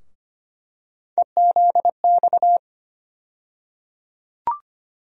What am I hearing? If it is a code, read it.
Morse code, heard as a keyed tone, about 25 wpm, 700 Hz. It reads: EZX
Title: USA Call Sign Postfixes - 3 Letters 25wpm